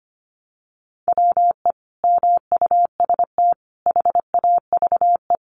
Morse code, heard as a keyed tone, about 25 wpm, 700 Hz. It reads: WE MUST HAVE
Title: Sentences from Top 1300 Words 25wpm